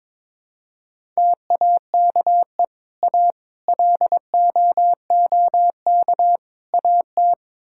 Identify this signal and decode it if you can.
Morse code, heard as a keyed tone, about 22 wpm, 700 Hz. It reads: TAKE A LOOK AT
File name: practice / 22wpm / Build A Phrase - 5-Word Phrases - 22wpm